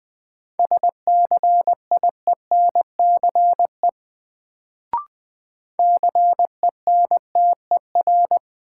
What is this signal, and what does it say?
Morse code, heard as a keyed tone, about 20 wpm, 700 Hz. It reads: SCIENCE CENTER
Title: Sentences from Top 1000 Words - Review Part 2 - New Word Drill - 20wpm